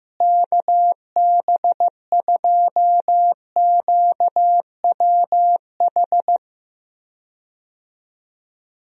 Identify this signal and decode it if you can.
Morse code, heard as a keyed tone, about 15 wpm, 700 Hz. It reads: KB2QWH
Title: USA Call Signs - 6 Characters - Repeated 3x - 15wpm